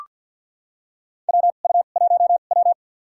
Morse code, heard as a keyed tone, about 47 wpm, spaced as for 35 wpm, 700 Hz. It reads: WU1W
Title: USA Call Signs - 2x Character Spacing - 50wpm